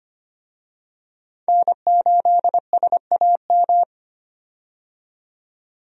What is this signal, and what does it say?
Morse code, heard as a keyed tone, about 25 wpm, 690 Hz. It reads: N8SAM
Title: USA Call Signs - 5 Characters - 25wpm